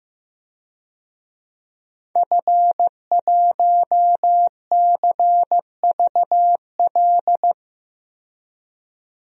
Morse code, heard as a keyed tone, about 15 wpm, 700 Hz. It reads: F1CVL
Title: DXCC-100 Call Signs - 5 Characters - 15wpm